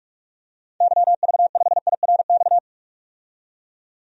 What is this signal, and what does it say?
Morse code, heard as a keyed tone, about 46 wpm, 700 Hz. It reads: YV5IRX
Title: DXCC-100 Call Signs - 6 Characters - 45wpm